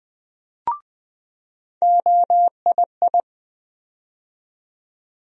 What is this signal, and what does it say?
Morse code, heard as a keyed tone, about 20 wpm, 700 Hz. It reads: OII
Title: USA Call Sign Postfixes - 3 Letters 20wpm